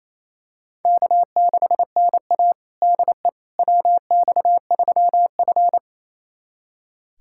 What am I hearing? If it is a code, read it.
Morse code, heard as a keyed tone, about 28 wpm, 700 Hz. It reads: K6NA DE WX3F
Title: USA Callsign DE Callsign - 4 Characters - 28wpm